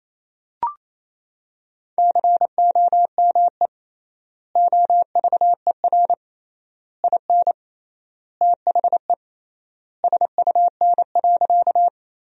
Morse code, heard as a keyed tone, about 28 wpm, 700 Hz. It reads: COME OVER IN THE SUN.
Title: 3x Word-Spacing - Sentences from Top 300 Words - 28wpm